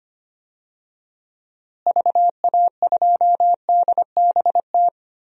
Morse code, heard as a keyed tone, about 25 wpm, 700 Hz. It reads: VA2DBT
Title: DXCC-100 Call Signs - 6 Characters - Repeated 2x - 25wpm